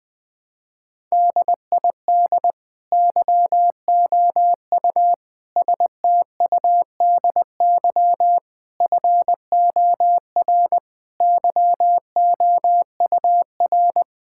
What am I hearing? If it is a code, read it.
Morse code, heard as a keyed tone, about 20 wpm, 700 Hz. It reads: DID YOU STUDY FOR YOUR
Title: Sentences from Top 1000 Words - Review Part 6 - 20wpm